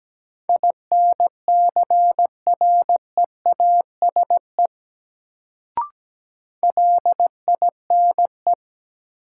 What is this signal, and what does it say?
Morse code, heard as a keyed tone, about 17 wpm, 700 Hz. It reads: INCREASE LINE
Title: Sentences from Top 1200 Words - New Word Drill Alt - 17wpm